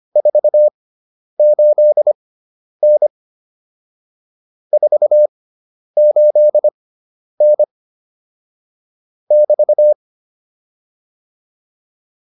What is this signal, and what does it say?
Morse code, heard as a keyed tone, about 25 wpm, spaced as for 10 wpm, 600 Hz. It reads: 48N 48N <BT>